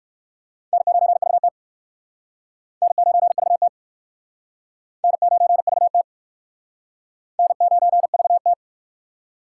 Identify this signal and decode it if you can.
Morse code, heard as a keyed tone, about 61 wpm, 700 Hz. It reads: N9VT N9VT N9VT N9VT